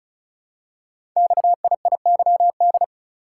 Morse code, heard as a keyed tone, about 36 wpm, 700 Hz. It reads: XIIYD
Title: Code Groups - 5 Letter:Numbers - 35wpm